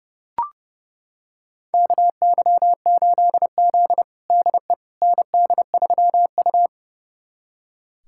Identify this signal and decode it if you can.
Morse code, heard as a keyed tone, about 32 wpm, 700 Hz. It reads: KY8Z DE ND3U
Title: USA Callsign DE Callsign - 4 Characters - 30wpm